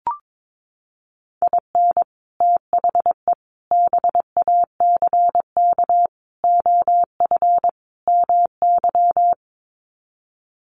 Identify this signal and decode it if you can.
Morse code, heard as a keyed tone, about 22 wpm, 700 Hz. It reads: IN THE BACK OF MY
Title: Sets of 5 Words Encore 22wpm